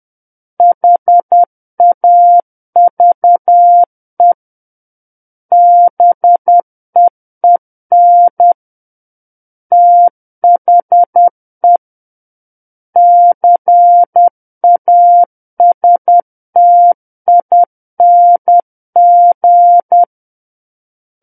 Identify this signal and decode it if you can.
Morse code, heard as a keyed tone, about 10 wpm, 700 Hz. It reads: HAVE BEEN THE CASTING
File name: Book One / 01-Chapter 1 - 10 WPM